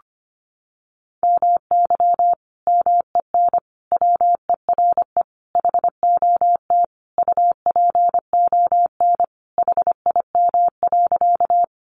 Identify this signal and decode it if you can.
Morse code, heard as a keyed tone, about 25 wpm, 700 Hz. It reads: MY MEN WERE HOT UPON HIM.
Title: Sentences from Top 1100 Words 25wpm